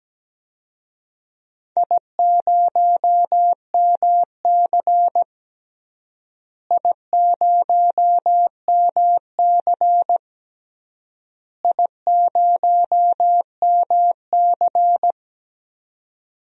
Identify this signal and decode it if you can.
Morse code, heard as a keyed tone, about 17 wpm, 700 Hz. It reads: I0MC I0MC I0MC
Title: DXCC-100 Call Signs - 4 Characters - Repeated 3x - 17wpm